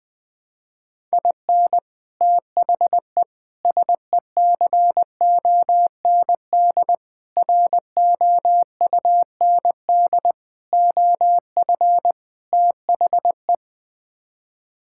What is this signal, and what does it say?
Morse code, heard as a keyed tone, about 20 wpm, 700 Hz. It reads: IN THE SECOND ROUND OF THE
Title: Sets of 6 Words - 20wpm